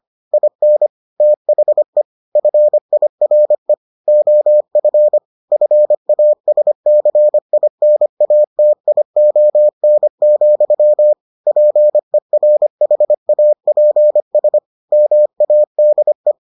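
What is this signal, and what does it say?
Morse code, heard as a keyed tone, about 25 wpm, 600 Hz. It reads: IN THE FIRE OF FASCINATION, PERHAPS MADE
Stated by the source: Morse edition of the Foundations of Amateur Radio episodes.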